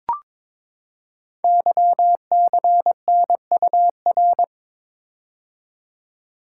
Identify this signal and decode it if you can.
Morse code, heard as a keyed tone, about 22 wpm, 700 Hz. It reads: YCNUR
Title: Code Groups - 5 Letter:Numbers - 22wpm